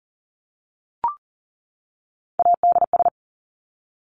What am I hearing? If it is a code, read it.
Morse code, heard as a keyed tone, about 40 wpm, 700 Hz. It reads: ADS
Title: Abbreviations 40wpm